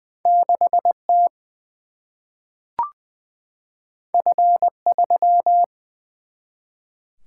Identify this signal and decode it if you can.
Morse code, heard as a keyed tone, about 20 wpm, 700 Hz. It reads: 6 T F 3
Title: Single-Letter Single-Number 20wpm